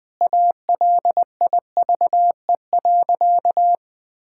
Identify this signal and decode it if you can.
Morse code, heard as a keyed tone, about 20 wpm, 700 Hz. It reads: ALIVE.
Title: Sentences with 5-Word Phrases - 20wpm